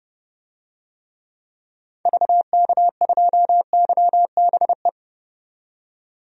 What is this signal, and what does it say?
Morse code, heard as a keyed tone, about 32 wpm, 700 Hz. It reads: VK2YBE
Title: DXCC-100 Call Signs - 6 Characters - 30wpm